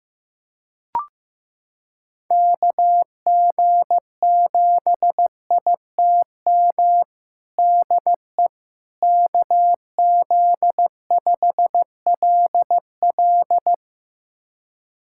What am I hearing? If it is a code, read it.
Morse code, heard as a keyed tone, about 15 wpm, 700 Hz. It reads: KG7ITM DE KZ5LL
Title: USA Callsign DE Callsign 15wpm